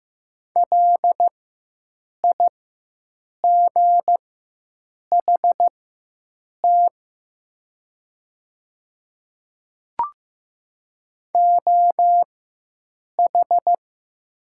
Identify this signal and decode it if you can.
Morse code, heard as a keyed tone, about 15 wpm, spaced as for 7 wpm, 700 Hz. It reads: LIGHT OH
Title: 4x Character Spacing - Top 500 Words - 15wpm